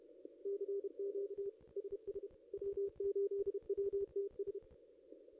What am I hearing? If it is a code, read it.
Morse code, heard as a keyed tone, about 31 wpm, 440 Hz. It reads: CQ SS W8WTS
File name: contests / ss-cw-k5tr-op-n5rz -0100